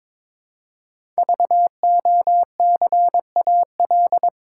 Morse code, heard as a keyed tone, about 22 wpm, 700 Hz. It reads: VOCAL
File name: 5 Letter Words - First 3 Letters Given - 22wpm